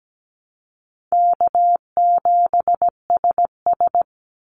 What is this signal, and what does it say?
Morse code, heard as a keyed tone, about 17 wpm, 700 Hz. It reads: K7SS
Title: Easy Call Signs - 17wpm